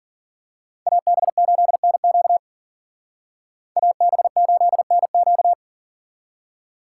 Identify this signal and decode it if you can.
Morse code, heard as a keyed tone, about 42 wpm, 700 Hz. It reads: AB8NQ AB8NQ
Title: Speed Racing - USA 2x2 Call Signs 40wpm